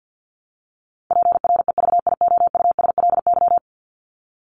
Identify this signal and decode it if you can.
Morse code, heard as a keyed tone, about 50 wpm, 700 Hz. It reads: PREVIOUSLY
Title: LY Suffix - Top 100 Words with Base Words 50wpm